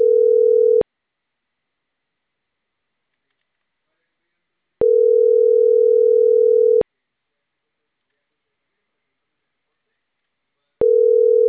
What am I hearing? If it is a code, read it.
Ringback tone (North America), 440 + 480 Hz.
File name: CENTRAL CAEA / out agcaea01 - .79009